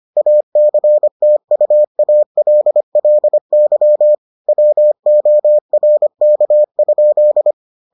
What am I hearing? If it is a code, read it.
Morse code, heard as a keyed tone, about 25 wpm, 600 Hz. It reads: ACTUALLY WORK?